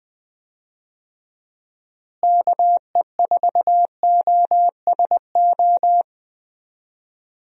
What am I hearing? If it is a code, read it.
Morse code, heard as a keyed tone, about 20 wpm, 700 Hz. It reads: KE4OSO
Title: USA Call Signs - 6 Characters - Repeated 3x - 20wpm